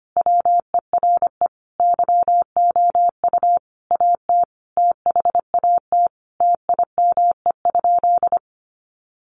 Morse code, heard as a keyed tone, about 25 wpm, 700 Hz. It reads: WERE YOU AT THAT TIME?
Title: Sentences from Top 200 Words 25wpm